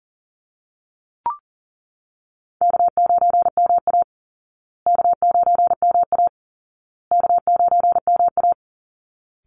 Morse code, heard as a keyed tone, about 40 wpm, 700 Hz. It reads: K9MA K9MA K9MA
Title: Call Signs Repeated 2x - 40wpm